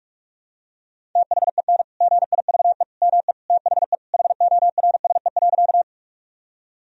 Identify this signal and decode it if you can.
Morse code, heard as a keyed tone, about 46 wpm, 700 Hz. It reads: THEN GIVE ME THE HORSE.
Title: Sentences from Top 1000 Words - Review Part 1 - 45wpm